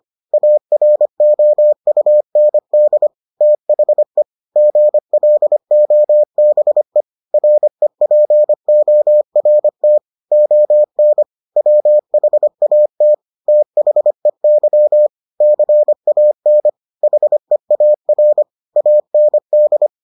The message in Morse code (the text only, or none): AROUND THE GLOBE REPORT ON WHAT THEY CAN HEAR AND